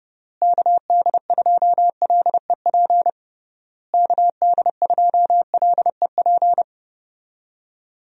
Morse code, heard as a keyed tone, about 30 wpm, 700 Hz. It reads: KD2LEP KD2LEP
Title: USA Call Signs - 6 Characters - Repeated 3x - 30wpm